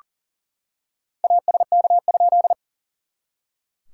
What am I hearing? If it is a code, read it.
Morse code, heard as a keyed tone, about 40 wpm, 690 Hz. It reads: ASK?
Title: SOTA QSO Practice 40wpm